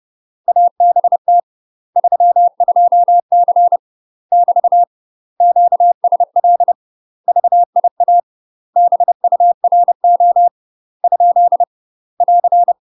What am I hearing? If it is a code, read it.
Morse code, heard as a keyed tone, about 30 wpm, 700 Hz. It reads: ABT 32C <BT> QSL VIA BURO ? <AR>
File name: UK MASTER QSOs TRAINING / 30WPM / UK MASTER QSOs 0050